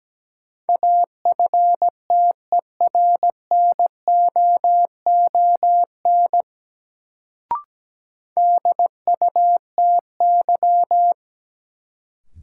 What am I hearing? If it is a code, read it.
Morse code, heard as a keyed tone, about 17 wpm, 700 Hz. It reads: AFTERNOON DUTY
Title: Sentences from Top 1500 Words - New Word Drill - 17wpm